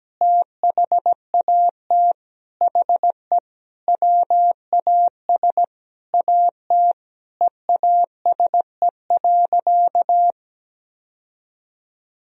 Morse code, heard as a keyed tone, about 17 wpm, 700 Hz. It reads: THAT HE WAS AT EASE.
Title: Sentences from Top 1000 Words - Review Part 1 - 17wpm